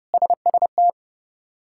Morse code, heard as a keyed tone, about 30 wpm, 700 Hz. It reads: SST
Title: K1USN SST QSO Practice - 30wpm